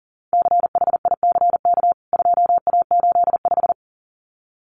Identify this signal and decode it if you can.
Morse code, heard as a keyed tone, about 40 wpm, 700 Hz. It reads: CHICK 2A85
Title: CWT Exchange 40wpm